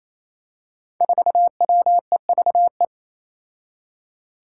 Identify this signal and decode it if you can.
Morse code, heard as a keyed tone, about 28 wpm, 700 Hz. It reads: 4WEVE
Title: Five Challenging Letter-Number Combinations 28wpm